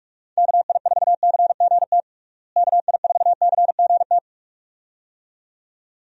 Morse code, heard as a keyed tone, about 46 wpm, 700 Hz. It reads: KI4CGT KI4CGT
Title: USA Call Signs - 6 Characters - Repeated 2x - 45wpm